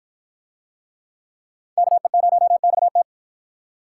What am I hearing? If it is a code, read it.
Morse code, heard as a keyed tone, about 53 wpm, 700 Hz. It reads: KE0XT